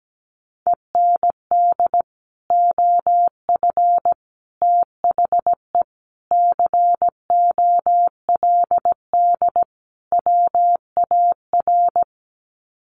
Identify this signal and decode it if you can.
Morse code, heard as a keyed tone, about 17 wpm, 700 Hz. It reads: END OF THE COLD WAR
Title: Sets of 5 Words Encore 17wpm